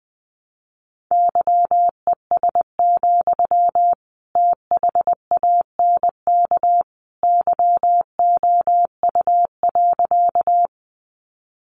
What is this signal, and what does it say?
Morse code, heard as a keyed tone, about 19 wpm, 700 Hz. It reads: YES, THANK YOU.
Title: Sentences from Top 600 Words 20wpm